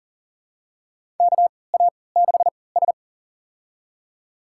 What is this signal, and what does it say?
Morse code, heard as a keyed tone, about 40 wpm, spaced as for 23 wpm, 700 Hz. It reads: KA6S
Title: USA Call Signs - 4 Characters - 3x Character Spacing - 40wpm